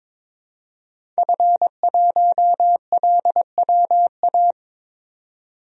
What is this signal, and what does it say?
Morse code, heard as a keyed tone, about 22 wpm, 700 Hz. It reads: F1LWA
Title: DXCC-100 Call Signs - 5 Characters - 22wpm